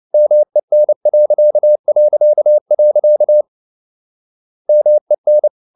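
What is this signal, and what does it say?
Morse code, heard as a keyed tone, about 29 wpm, 600 Hz. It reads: MEN... MEN
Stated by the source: Fast (QRQ) Morse at 29 wpm, from CW med Gnister.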